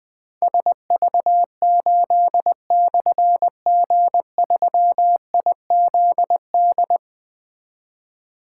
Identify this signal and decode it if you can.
Morse code, heard as a keyed tone, about 20 wpm, 700 Hz. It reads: SV8/G3IZD
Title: Difficult Call Signs 20wpm